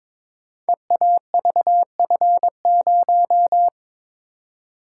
Morse code, heard as a keyed tone, about 22 wpm, 700 Hz. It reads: EAVF0
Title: Code Groups - 5 Letter:Numbers - 22wpm